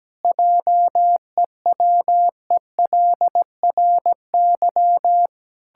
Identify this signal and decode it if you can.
Morse code, heard as a keyed tone, about 17 wpm, 700 Hz. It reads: JEWELRY
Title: Pangram Challenge v2 - 17wpm